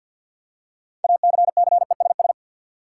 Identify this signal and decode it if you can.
Morse code, heard as a keyed tone, about 53 wpm, 700 Hz. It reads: ACCESS
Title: Six Letter Words - 50wpm